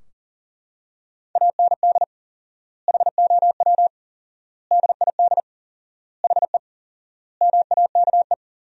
Morse code, heard as a keyed tone, about 40 wpm, 700 Hz. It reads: AND HOW DID HE MAKE
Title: 4x Word-Spacing - Sentences from Top 400 Words - 40wpm